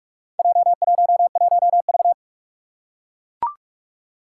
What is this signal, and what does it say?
Morse code, heard as a keyed tone, about 44 wpm, 700 Hz. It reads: J11V
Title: Four Challenging Letter-Number Combinations 45wpm